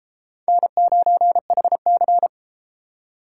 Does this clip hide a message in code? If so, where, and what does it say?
Yes: Morse code from 0.5 to 2.3 s, heard as a keyed tone, about 31 wpm, 700 Hz. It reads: N9HC